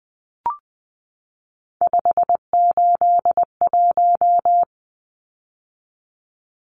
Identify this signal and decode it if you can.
Morse code, heard as a keyed tone, about 20 wpm, 700 Hz. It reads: 581
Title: Numbers - 3-Digit 20wpm